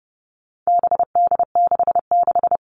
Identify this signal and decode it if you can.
Morse code, heard as a keyed tone, about 30 wpm, 700 Hz. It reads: BD66
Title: All Cut Numbers - 4-Digit 30wpm